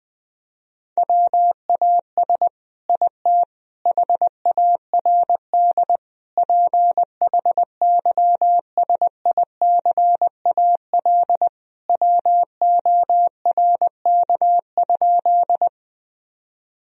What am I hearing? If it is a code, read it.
Morse code, heard as a keyed tone, about 20 wpm, 700 Hz. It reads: WAS IT HARD PHYSICAL WORK?
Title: Sentences from Top 1000 Words - Review Part 4 - 20wpm